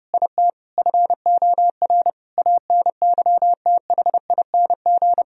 Morse code, heard as a keyed tone, about 30 wpm, 700 Hz. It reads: IT FOR ANYTHING
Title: Sentences from Top 1000 Words - Review Part 3 - 30wpm